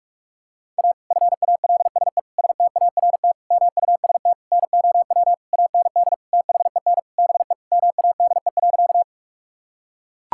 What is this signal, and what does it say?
Morse code, heard as a keyed tone, about 41 wpm, 700 Hz. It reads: A FALSE START MUST NOW AND THEN BE MADE.
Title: Sentences from Top 1600 Words 45wpm